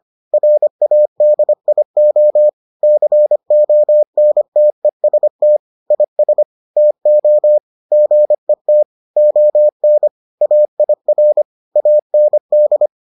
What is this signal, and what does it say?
Morse code, heard as a keyed tone, about 25 wpm, 600 Hz. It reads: RADIO CONTEST IS TO GET ON AIR AND